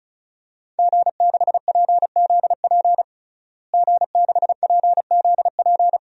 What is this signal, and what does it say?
Morse code, heard as a keyed tone, about 36 wpm, 700 Hz. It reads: G6PZP G6PZP
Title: DXCC-100 Call Signs - 5 Characters - Repeated 2x - 35wpm